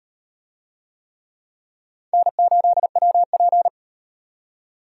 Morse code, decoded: N8WP

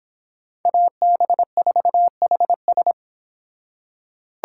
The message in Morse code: AB4HS

